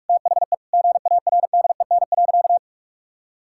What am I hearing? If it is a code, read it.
Morse code, heard as a keyed tone, about 46 wpm, 700 Hz. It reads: THE GARDEN.
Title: Sentences from Top 1000 Words - Review Part 6 - 45wpm